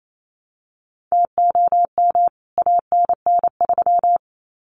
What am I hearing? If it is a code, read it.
Morse code, heard as a keyed tone, about 28 wpm, 700 Hz. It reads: TOM ANN3
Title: CWT Exchange 28wpm